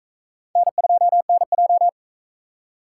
Morse code, heard as a keyed tone, about 37 wpm, 700 Hz. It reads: N2NJ